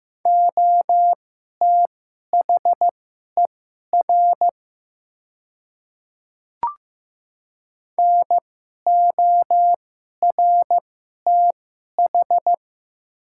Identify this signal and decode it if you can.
Morse code, heard as a keyed tone, about 15 wpm, spaced as for 11 wpm, 700 Hz. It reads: OTHER NORTH
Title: 2x Character Spacing - Top 300 Words - 15wpm